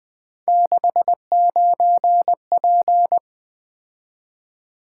Morse code, heard as a keyed tone, about 20 wpm, 700 Hz. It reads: 69P
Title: Three Letter-Number Combinations - 20wpm